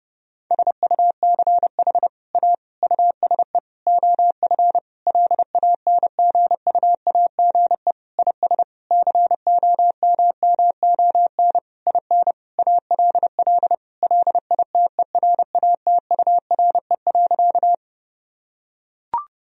Morse code, heard as a keyed tone, about 32 wpm, 700 Hz. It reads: SUCH A USE OF LANGUAGE IS COMMON IN ALL LITERATURE.
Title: Sentences from Top 1000 Words - Review Part 3 - 30wpm